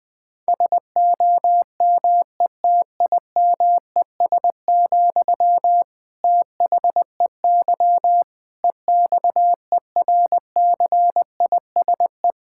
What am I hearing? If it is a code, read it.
Morse code, heard as a keyed tone, about 20 wpm, 700 Hz. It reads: SOMETIMES, THEY EXERCISE
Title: Sentences from Top 1000 Words - Review Part 6 - 20wpm